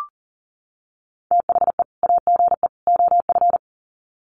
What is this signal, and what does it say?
Morse code, heard as a keyed tone, about 40 wpm, 700 Hz. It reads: THE AGE OF
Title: Sets of 3 Words 40wpm